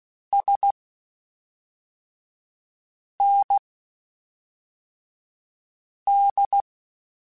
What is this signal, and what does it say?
Morse code, heard as a keyed tone, about 16 wpm, spaced as for 3 wpm, 800 Hz. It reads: SND
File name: (spaced) / QSOs Spaced-mp3s / WA7QOX 1 @5 sp